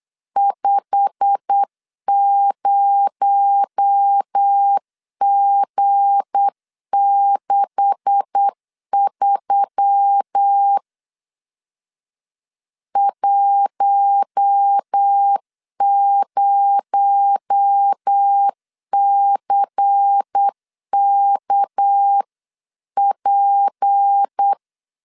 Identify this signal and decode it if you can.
Morse code, heard as a keyed tone, about 8 wpm, 800 Hz. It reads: 50G63 10CKP
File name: CW / DET / 1 9 1500C S B 8wpm 5624